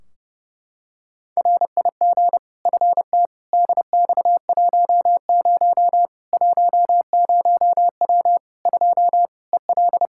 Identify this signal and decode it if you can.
Morse code, heard as a keyed tone, about 30 wpm, 700 Hz. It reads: RIG FT DX10 10W 2 EL
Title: K1USN SST QSO Practice 30wpm